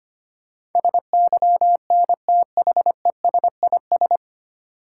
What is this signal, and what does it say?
Morse code, heard as a keyed tone, about 25 wpm, 690 Hz. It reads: SYNTHESIS
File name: practice / 25wpm / 9 Letter Words - First 3 Letters Given - 25wpm